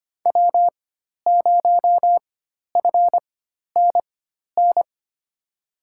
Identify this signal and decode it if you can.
Morse code, heard as a keyed tone, about 25 wpm, spaced as for 12 wpm, 700 Hz. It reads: W0FNN
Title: USA Call Signs - 5 Characters - 4x Character Spacing - 25wpm